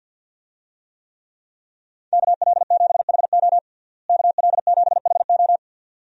Morse code, heard as a keyed tone, about 47 wpm, 700 Hz. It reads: KL7HO KL7HO
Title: DXCC-100 Call Signs - 5 Characters - Repeated 2x - 50wpm